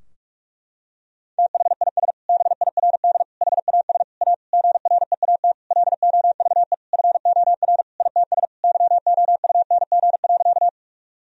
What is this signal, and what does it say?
Morse code, heard as a keyed tone, about 42 wpm, 700 Hz. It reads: THIS BIRD HAS A GREAT LOVE FOR ITS YOUNG.
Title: Sentences from Top 1000 Words - Review Part 2 - 45wpm